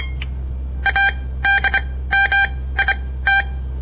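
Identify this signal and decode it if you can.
Morse code, heard as a keyed tone, about 25 wpm, spaced as for 16 wpm, 1600 Hz. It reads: ADMIT
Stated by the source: On-air Morse code practice broadcast by a southern Montana amateur radio association; all code, no read back.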